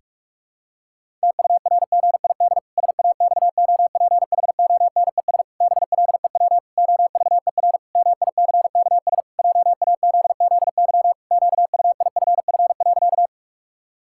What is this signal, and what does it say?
Morse code, heard as a keyed tone, about 48 wpm, 700 Hz. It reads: TURGID SAXOPHONES BLEW OVER MICKS JAZZY QUIFF.
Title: Pangram Challenge v2 - 45wpm